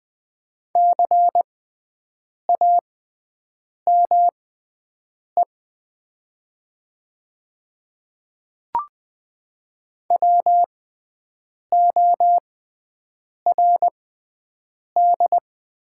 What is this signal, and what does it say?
Morse code, heard as a keyed tone, about 20 wpm, spaced as for 7 wpm, 700 Hz. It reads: CAME WORD
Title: 6x Character Spacing - Top 300 Words - 20wpm